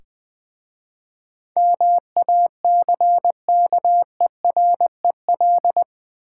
Morse code, heard as a keyed tone, about 20 wpm, 700 Hz. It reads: MACKEREL
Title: Difficult Words with Hints - 20wpm